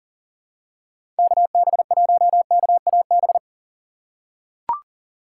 Morse code, heard as a keyed tone, about 40 wpm, 700 Hz. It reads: KB1KAB
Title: USA Call Signs - 6 Characters - 2x Character Spacing - 40wpm